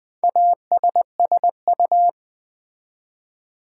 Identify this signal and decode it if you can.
Morse code, heard as a keyed tone, about 20 wpm, 700 Hz. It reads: ASSU
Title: Build A Word - 2 Letters At A Time - 6-Letter Words - 20wpm